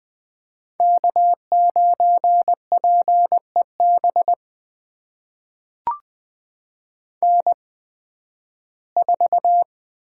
Morse code, heard as a keyed tone, about 20 wpm, spaced as for 13 wpm, 700 Hz. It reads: K9PEB N4
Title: USA Call Signs - 5 Characters - 8x Character Spacing - 20wpm